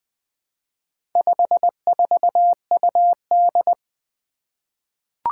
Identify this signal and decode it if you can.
Morse code, heard as a keyed tone, about 20 wpm, 700 Hz. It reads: 54UD
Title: Four Challenging Letter-Number Combinations 20wpm